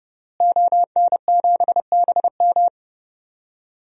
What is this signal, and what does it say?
Morse code, heard as a keyed tone, about 30 wpm, 700 Hz. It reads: ON7BM
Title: DXCC-100 Call Signs - 5 Characters - Repeated 3x - 30wpm